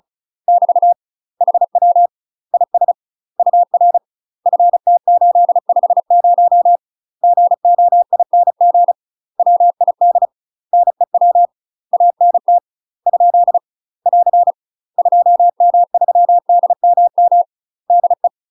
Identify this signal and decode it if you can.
Morse code, heard as a keyed tone, about 36 wpm, 700 Hz. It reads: <BT> HW IS UR FT850 GOING WID NEW ANT ? <AR> 2M3DMM DE
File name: UK MASTER QSOs TRAINING / 35WPM / UK MASTER QSOs 0009